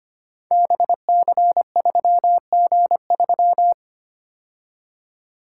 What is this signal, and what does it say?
Morse code, heard as a keyed tone, about 25 wpm, 700 Hz. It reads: BC3G3
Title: Code Groups - 5 Letter:Numbers - 25wpm